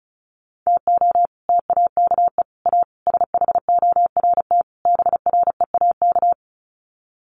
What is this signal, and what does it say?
Morse code, heard as a keyed tone, about 34 wpm, 700 Hz. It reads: TO TAKE A SHORT BREAK
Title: Sets of 5 Words Encore2 35wpm